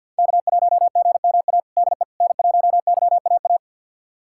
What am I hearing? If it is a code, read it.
Morse code, heard as a keyed tone, about 51 wpm, 690 Hz. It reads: K1GMA DE N1YAA
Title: USA Callsign DE Callsign - 5 Characters - 50wpm